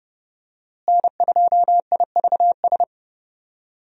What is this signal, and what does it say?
Morse code, heard as a keyed tone, about 31 wpm, 700 Hz. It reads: N2IVS
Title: USA Call Signs - 5 Characters - Repeated 2x - 30wpm